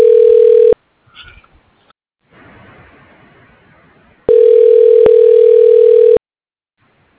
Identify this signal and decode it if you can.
Ringback tone (North America), 440 + 480 Hz.